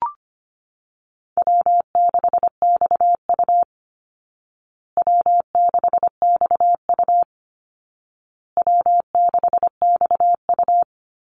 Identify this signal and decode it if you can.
Morse code, heard as a keyed tone, about 25 wpm, 700 Hz. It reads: W6XU W6XU W6XU
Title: Easy Call Signs Repeated 3x - 25wpm